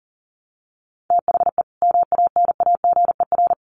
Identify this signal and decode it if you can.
Morse code, heard as a keyed tone, about 40 wpm, 700 Hz. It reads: THE MANAGER
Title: Additional Easy Phrases 40wpm